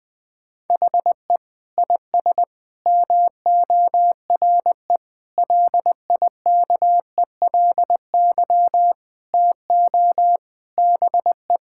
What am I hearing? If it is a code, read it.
Morse code, heard as a keyed tone, about 20 wpm, 700 Hz. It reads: HE IS MORE LIKELY TO BE
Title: Sentences with 4-Word Phrases - 20wpm